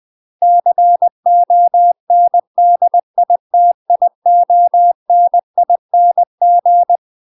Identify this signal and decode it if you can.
Morse code, heard as a keyed tone, about 20 wpm, 700 Hz. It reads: CONDITIONING